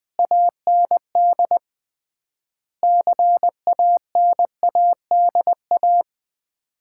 Morse code, heard as a keyed tone, about 20 wpm, 690 Hz. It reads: AND CANADA
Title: 3x Word-Spacing - Sets of 5 Words - 20wpm